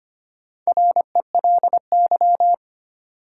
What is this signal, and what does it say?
Morse code, heard as a keyed tone, about 25 wpm, 700 Hz. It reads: RELY
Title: Sentences from Top 1200 Words - New Word Drill Alt - 25wpm